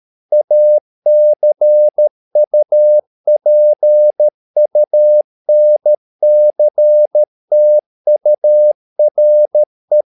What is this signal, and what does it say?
Morse code, heard as a keyed tone, about 13 wpm, 600 Hz. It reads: ACUPUNCTURE